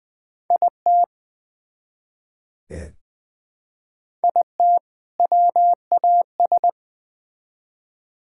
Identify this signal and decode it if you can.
Morse code, heard as a keyed tone, about 20 wpm, 700 Hz. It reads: IT IT WAS
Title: Build A Phrase - 2-Word Phrases - 20wpm